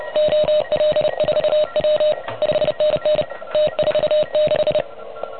Morse code, heard as a keyed tone, about 32 wpm, 600 Hz. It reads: OL4W 5NN T46
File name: truSDX / MWC29.8